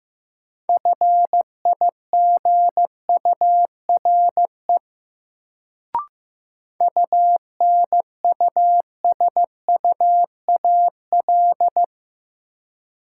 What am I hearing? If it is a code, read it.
Morse code, heard as a keyed tone, about 15 wpm, 700 Hz. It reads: FIGURE UNUSUAL
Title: Sentences from Top 1400 Words - New Word Drill Alt - 15wpm